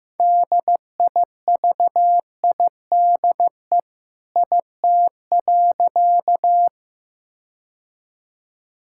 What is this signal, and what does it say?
Morse code, heard as a keyed tone, about 15 wpm, 700 Hz. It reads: DIVIDE IT.
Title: Sentences from Top 1600 Words 15wpm